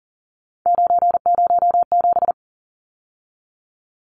Morse code, heard as a keyed tone, about 40 wpm, 700 Hz. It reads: 907
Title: Numbers - 3-Digit 40wpm